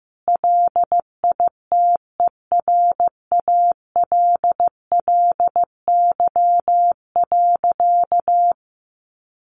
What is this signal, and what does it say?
Morse code, heard as a keyed tone, about 15 wpm, 700 Hz. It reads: LITERALLY.
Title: Sentences from Top 1400 Words 15wpm